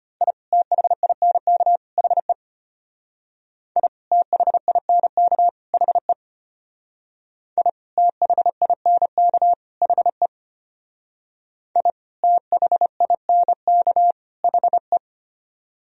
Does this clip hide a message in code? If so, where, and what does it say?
Yes: Morse code from 0.2 to 15.0 s, heard as a keyed tone, about 33 wpm, 700 Hz. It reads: I THINK HE I THINK HE I THINK HE I THINK HE